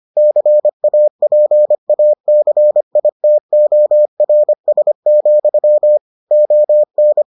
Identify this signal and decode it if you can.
Morse code, heard as a keyed tone, about 25 wpm, 600 Hz. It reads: CAPACITORS, ON